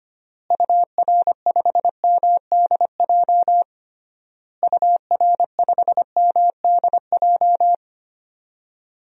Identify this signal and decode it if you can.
Morse code, heard as a keyed tone, about 25 wpm, 700 Hz. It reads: UR5MDJ UR5MDJ
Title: DXCC-100 Call Signs - 6 Characters - Repeated 3x - 25wpm